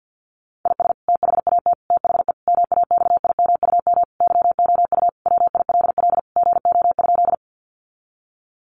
Morse code, heard as a keyed tone, about 50 wpm, 700 Hz. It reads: IS THAT THE MAXIMUM YOU WILL GO?
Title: Sentences from Top 1300 Words 50wpm